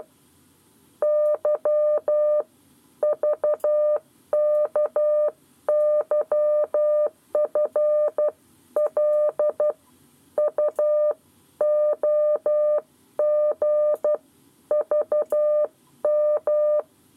Morse code, heard as a keyed tone, about 12 wpm, 600 Hz. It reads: Y VKYFL UOG VM
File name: 7 Lezione cw 1